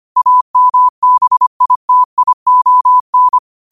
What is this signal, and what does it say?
Morse code, heard as a keyed tone, about 25 wpm, 1000 Hz. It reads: AMBITION